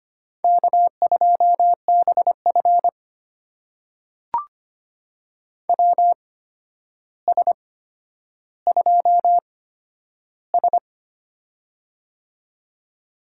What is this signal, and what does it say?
Morse code, heard as a keyed tone, about 25 wpm, spaced as for 11 wpm, 700 Hz. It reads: K2BF WS2S
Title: USA Call Signs - 4 Characters - 8x Character Spacing - 25wpm